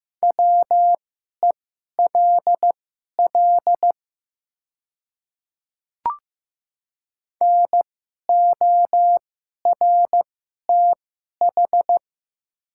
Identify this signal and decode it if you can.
Morse code, heard as a keyed tone, about 15 wpm, spaced as for 11 wpm, 700 Hz. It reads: WELL NORTH
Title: 2x Character Spacing - Top 300 Words - 15wpm